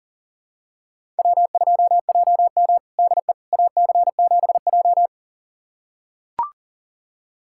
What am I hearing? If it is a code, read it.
Morse code, heard as a keyed tone, about 40 wpm, 700 Hz. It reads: W2JM DE AC7J
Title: USA Callsign DE Callsign 40wpm